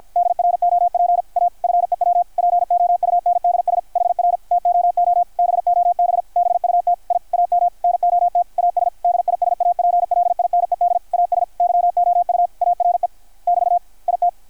Morse code, heard as a keyed tone, about 52 wpm, 700 Hz, with radio noise. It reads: DROP A FEW POUNDS HR TOO BOB BUT I AM NOT AS DISAPLINED AS YOU ARE <BT> IT
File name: archived / CRSnet